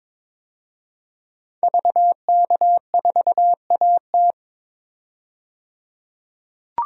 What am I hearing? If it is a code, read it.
Morse code, heard as a keyed tone, about 22 wpm, 700 Hz. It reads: VK4AT
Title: DXCC-100 Call Signs - 5 Characters - 22wpm